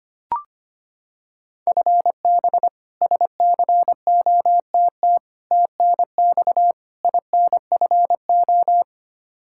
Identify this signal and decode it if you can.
Morse code, heard as a keyed tone, about 25 wpm, 700 Hz. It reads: FB SCOTT TNX INFO
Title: K1USN SST QSO Practice 25wpm